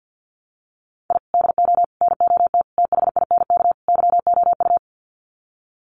Morse code, heard as a keyed tone, about 48 wpm, 700 Hz. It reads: I DO NOT THINK YOU
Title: Sets of 5 Words 50wpm